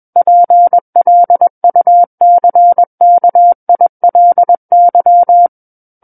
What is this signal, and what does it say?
Morse code, heard as a keyed tone, about 21 wpm, 700 Hz. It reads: PLUCKILY
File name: Book One / 12-Chapter 12 - 21 WPM